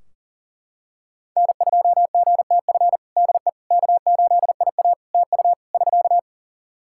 Morse code, heard as a keyed tone, about 40 wpm, 700 Hz. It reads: N2GTF DE K8IA TU <SK>
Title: Practice QSO - Exchange 4 - Ending - 40wpm